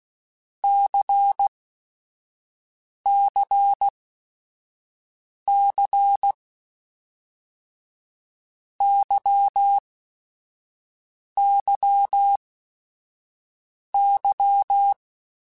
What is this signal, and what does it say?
Morse code, heard as a keyed tone, about 16 wpm, spaced as for 5 wpm, 800 Hz. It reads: CCC YYY
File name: Basic Sets / Basic Letter Groups Collective7 / LFCYQ@5